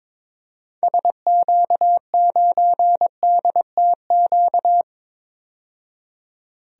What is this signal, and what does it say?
Morse code, heard as a keyed tone, about 22 wpm, 700 Hz. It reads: SQ9DTQ
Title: DXCC-100 Call Signs - 6 Characters - 22wpm